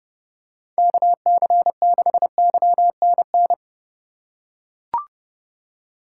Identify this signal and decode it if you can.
Morse code, heard as a keyed tone, about 32 wpm, 700 Hz. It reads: KC6YNN
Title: USA Call Signs - 3x Character Spacing - 30wpm